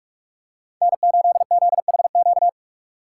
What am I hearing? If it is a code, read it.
Morse code, heard as a keyed tone, about 46 wpm, 700 Hz. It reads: N8ZHQ
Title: USA Call Signs - 5 Characters - 3x Character Spacing - 45wpm